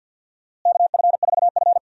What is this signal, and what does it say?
Morse code, heard as a keyed tone, about 46 wpm, 700 Hz. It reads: KF4F
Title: USA Call Signs - 4 Characters - Repeated 3x - 50wpm